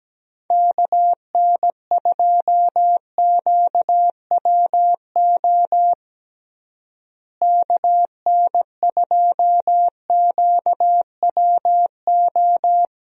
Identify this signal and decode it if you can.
Morse code, heard as a keyed tone, about 17 wpm, 700 Hz. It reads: KN2QWO KN2QWO
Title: USA Call Signs - 6 Characters - Repeated 2x - 17wpm